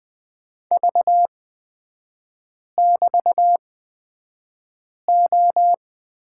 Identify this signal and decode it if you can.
Morse code, heard as a keyed tone, about 20 wpm, spaced as for 5 wpm, 700 Hz. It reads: V<BT>O
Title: 224 Lesson 36e - BT - Three Characters 20wpm